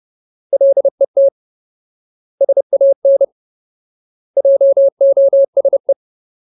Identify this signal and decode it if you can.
Morse code, heard as a keyed tone, about 30 wpm, 550 Hz. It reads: LET SAN JOSE
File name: Ultimate Word List / 30WPM Ultimate